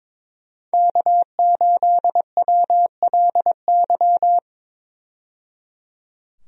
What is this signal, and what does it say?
Morse code, heard as a keyed tone, about 22 wpm, 700 Hz. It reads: K8WLY
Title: USA Call Signs - 5 Characters - 22wpm